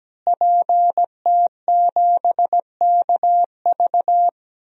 Morse code, heard as a keyed tone, about 17 wpm, 700 Hz. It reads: PT7KV
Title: DXCC-100 Call Signs - 5 Characters - Repeated 3x - 17wpm